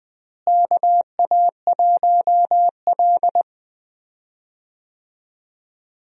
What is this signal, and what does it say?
Morse code, heard as a keyed tone, about 20 wpm, 700 Hz. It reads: KA1L
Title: USA Call Signs - 4 Characters - Repeated 2x - 20wpm